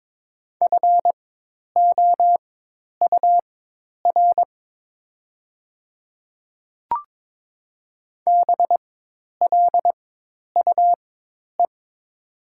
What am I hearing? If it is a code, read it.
Morse code, heard as a keyed tone, about 21 wpm, spaced as for 10 wpm, 700 Hz. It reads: FOUR BLUE
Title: 4x Character Spacing - Top 500 Words - 22wpm